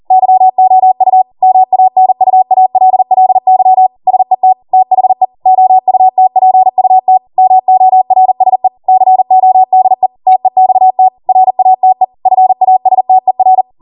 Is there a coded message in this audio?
Yes: Morse code from 0.1 to 13.6 s, heard as a keyed tone, about 40 wpm, 750 Hz. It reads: YOU MANUALLY SET THE OUTPUT MORSE CODE TEXT RATE FASTER